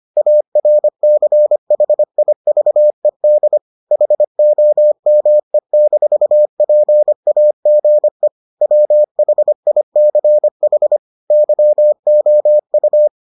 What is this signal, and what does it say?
Morse code, heard as a keyed tone, about 25 wpm, 600 Hz. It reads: ARCHIVED HOME-PAGE WHICH YOU